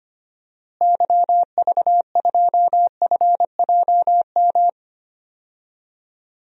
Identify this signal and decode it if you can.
Morse code, heard as a keyed tone, about 25 wpm, 700 Hz. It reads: YV2FJM
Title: DXCC-100 Call Signs - 6 Characters - Repeated 2x - 25wpm